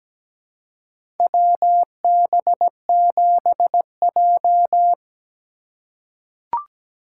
Morse code, heard as a keyed tone, about 17 wpm, 700 Hz. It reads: WB7J
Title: USA Call Signs - 4 Characters - 4x Character Spacing - 17wpm